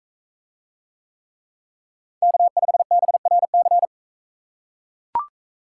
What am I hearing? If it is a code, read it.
Morse code, heard as a keyed tone, about 40 wpm, 700 Hz. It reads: K5BRC